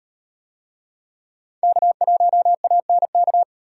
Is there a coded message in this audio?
Yes: Morse code from 1.6 to 3.4 s, heard as a keyed tone, about 38 wpm, 700 Hz. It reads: K1ANK